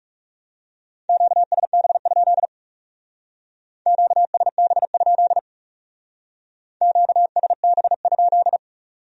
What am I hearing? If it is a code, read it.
Morse code, heard as a keyed tone, about 37 wpm, 700 Hz. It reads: QSB? QSB? QSB?
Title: Speed-Racing Q-Codes - 30wpm